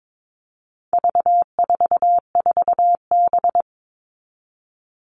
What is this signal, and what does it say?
Morse code, heard as a keyed tone, about 22 wpm, 700 Hz. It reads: V44B
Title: All Cut Numbers - 4-Digit 22wpm